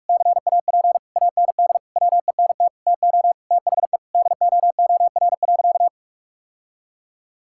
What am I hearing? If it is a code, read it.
Morse code, heard as a keyed tone, about 46 wpm, 700 Hz. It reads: CAP AND WENT TO THE DOOR.
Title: Sentences from Top 1600 Words 45wpm